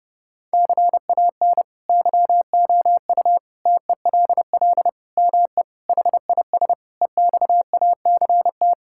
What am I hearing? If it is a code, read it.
Morse code, heard as a keyed tone, about 30 wpm, 700 Hz. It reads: CAN YOU TELL ME HIS EXACT
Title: Sentences from Top 1600 Words 30wpm